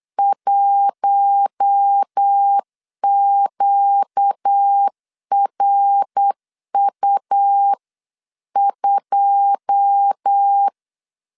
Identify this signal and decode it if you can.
Morse code, heard as a keyed tone, about 8 wpm, 800 Hz. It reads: 1QRU2
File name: CW / DET / 1 9 1500C S B 8wpm 5624